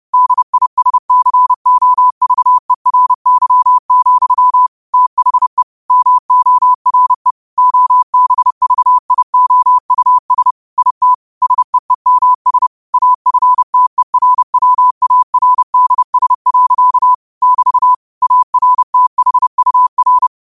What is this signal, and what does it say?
Morse code, heard as a keyed tone, about 30 wpm, 1000 Hz. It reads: DISCOVERY, THE MORE OBVIOUS IT SEEMS AFTERWARDS. <BT> ARTHUR
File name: qotdfiles / 30wpm